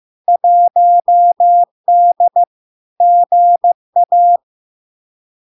Morse code, heard as a keyed tone, about 15 wpm, 700 Hz. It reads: 1D GA
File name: LARK-FD-2022-Log-in-CW-15wpm-700-Hz